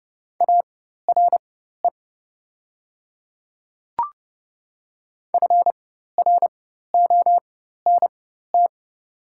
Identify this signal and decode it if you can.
Morse code, heard as a keyed tone, about 30 wpm, spaced as for 14 wpm, 690 Hz. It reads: ARE FRONT
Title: 4x Character Spacing - Top 500 Words - 30wpm